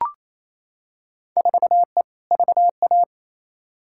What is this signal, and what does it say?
Morse code, heard as a keyed tone, about 28 wpm, 700 Hz. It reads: 4E VA
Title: Field Day Exchanges Encore - 28wpm